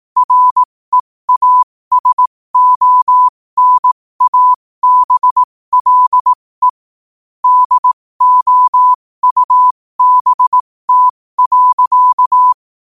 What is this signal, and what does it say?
Morse code, heard as a keyed tone, about 18 wpm, spaced as for 15 wpm, 1000 Hz. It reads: REASONABLE DOUBT.
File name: qotdfiles / 15wpm